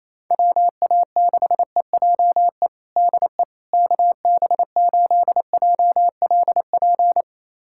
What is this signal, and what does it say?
Morse code, heard as a keyed tone, about 28 wpm, 700 Hz. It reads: WA6EJE DE KB8JLP
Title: USA Callsign DE Callsign - 6 Characters - 28wpm